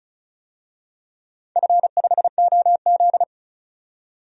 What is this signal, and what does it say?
Morse code, heard as a keyed tone, about 36 wpm, 700 Hz. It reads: F5OZ
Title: DXCC-100 Call Signs - 4 Characters - 35wpm